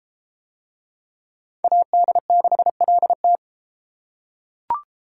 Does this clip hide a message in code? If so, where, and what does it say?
Yes: Morse code from 1.6 to 3.4 s, heard as a keyed tone, about 30 wpm, 700 Hz. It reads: AD6LT